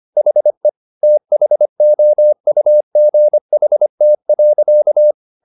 Morse code, heard as a keyed tone, about 25 wpm, 600 Hz. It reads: HE THOUGHT.
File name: the old man and the sea part 1 25wpm